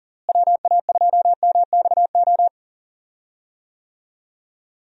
Morse code, heard as a keyed tone, about 40 wpm, 700 Hz. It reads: WA2MXO
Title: USA Call Signs - 6 Characters - Repeated 2x - 40wpm